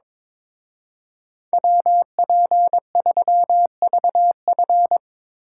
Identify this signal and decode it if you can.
Morse code, heard as a keyed tone, about 25 wpm, 700 Hz. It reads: WP3VF
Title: Speed Racing - USA 2x2 Call Signs 22wpm